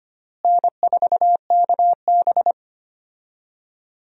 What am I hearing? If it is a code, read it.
Morse code, heard as a keyed tone, about 25 wpm, 700 Hz. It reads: N4KB
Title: USA Call Signs - 4 Characters - Repeated 2x - 25wpm